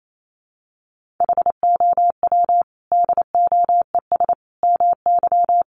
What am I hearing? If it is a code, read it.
Morse code, heard as a keyed tone, about 28 wpm, 700 Hz. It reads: HOW DOES MY
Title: Sentences from Top 1500 Words 28wpm